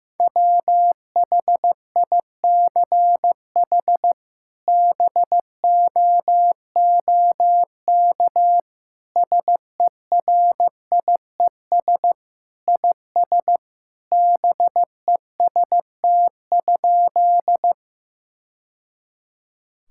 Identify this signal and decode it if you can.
Morse code, heard as a keyed tone, about 15 wpm, 700 Hz. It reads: WHICH BOOK SERIES IS BEST?
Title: Sentences from Top 1000 Words - Review Part 5 - 15wpm